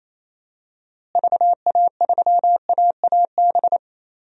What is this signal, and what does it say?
Morse code, heard as a keyed tone, about 28 wpm, 700 Hz. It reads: VA3AAB
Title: DXCC-100 Call Signs - 6 Characters - Repeated 3x - 28wpm